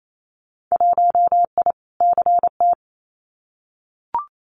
Morse code, heard as a keyed tone, about 28 wpm, 700 Hz. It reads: 1I CT
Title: Winter Field Day Exchanges 28wpm